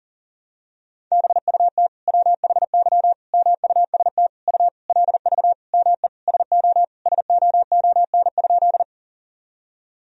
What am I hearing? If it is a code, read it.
Morse code, heard as a keyed tone, about 39 wpm, 700 Hz. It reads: BUT WHY MUST U LV ME SO SOON?
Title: Sentences from Top 400 Words and Abbreviations 40wpm